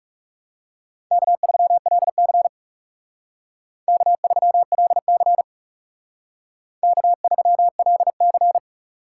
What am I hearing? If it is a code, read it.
Morse code, heard as a keyed tone, about 38 wpm, 700 Hz. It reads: K3LC K3LC K3LC